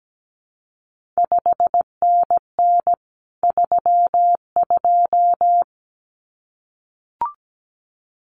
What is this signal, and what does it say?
Morse code, heard as a keyed tone, about 17 wpm, 700 Hz. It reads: 5NN 32
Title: CQ WW DX Contest Exchange Encore - 17wpm